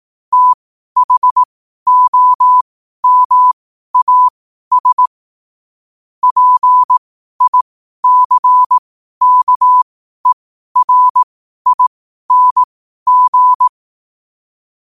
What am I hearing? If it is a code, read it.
Morse code, heard as a keyed tone, about 18 wpm, spaced as for 12 wpm, 1000 Hz. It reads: THOMAS PICKERING
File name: qotdfiles / 12wpm